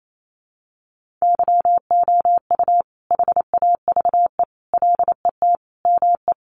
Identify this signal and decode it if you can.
Morse code, heard as a keyed tone, about 28 wpm, 700 Hz. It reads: YOU HAVE LET ME
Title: Sentences from Top 1500 Words 28wpm